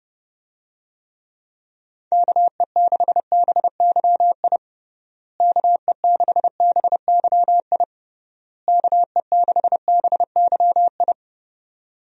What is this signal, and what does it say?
Morse code, heard as a keyed tone, about 32 wpm, 700 Hz. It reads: KE6BYI KE6BYI KE6BYI
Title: USA Call Signs - 6 Characters - Repeated 3x - 30wpm